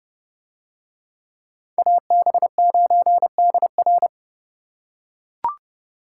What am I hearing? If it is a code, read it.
Morse code, heard as a keyed tone, about 29 wpm, 700 Hz. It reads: AB9DR